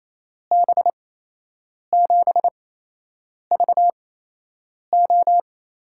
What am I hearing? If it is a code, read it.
Morse code, heard as a keyed tone, about 28 wpm, spaced as for 8 wpm, 700 Hz. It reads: B7VO
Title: USA Call Signs - 6 Characters - 8x Character Spacing - 28wpm